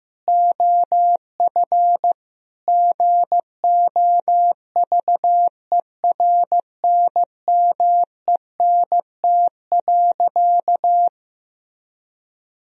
Morse code, heard as a keyed tone, about 16 wpm, 700 Hz. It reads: OF GOVERNMENT.
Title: Sentences from Top 1000 Words - Review Part 3 - 15wpm